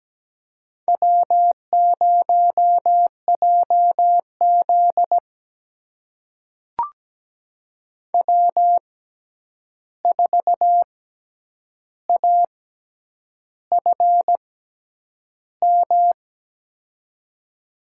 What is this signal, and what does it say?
Morse code, heard as a keyed tone, about 17 wpm, spaced as for 8 wpm, 700 Hz. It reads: W0JZ W4AFM
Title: USA Call Signs - 6x Character Spacing - 17wpm